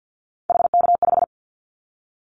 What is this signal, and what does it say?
Morse code, heard as a keyed tone, about 50 wpm, 700 Hz. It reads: HK5
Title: Callsign Prefixes for Top 100 DXCC - 50wpm